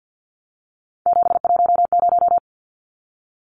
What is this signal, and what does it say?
Morse code, heard as a keyed tone, about 50 wpm, 700 Hz. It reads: 710
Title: Numbers - 3-Digit 50wpm